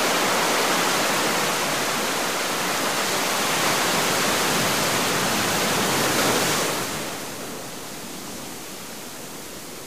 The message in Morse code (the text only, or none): none